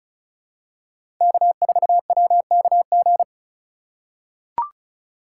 Morse code, heard as a keyed tone, about 34 wpm, 700 Hz. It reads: K4WKG
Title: USA Call Signs - 5 Characters - 3x Character Spacing - 35wpm